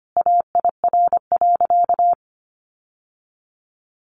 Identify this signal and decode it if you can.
Morse code, heard as a keyed tone, about 25 wpm, 700 Hz. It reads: AIR.
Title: Sentences from Top 900 Words 25wpm